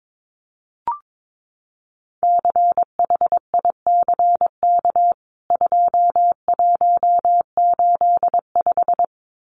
Morse code, heard as a keyed tone, about 22 wpm, 700 Hz. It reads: CHICK 2185
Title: CWT Exchange 22wpm